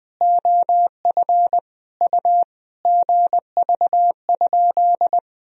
Morse code, heard as a keyed tone, about 20 wpm, 700 Hz. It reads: OF U GV?
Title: Sentences from Top 200 Words and Abbreviations 20wpm